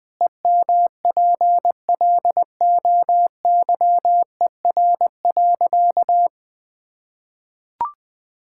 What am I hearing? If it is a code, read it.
Morse code, heard as a keyed tone, about 21 wpm, 700 Hz. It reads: EMPLOYER.
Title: Pangram Challenge v2 - 20wpm